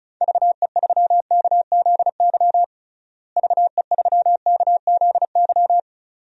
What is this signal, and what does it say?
Morse code, heard as a keyed tone, about 36 wpm, 700 Hz. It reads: VE3KZY VE3KZY
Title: DXCC-100 Call Signs - 6 Characters - Repeated 2x - 35wpm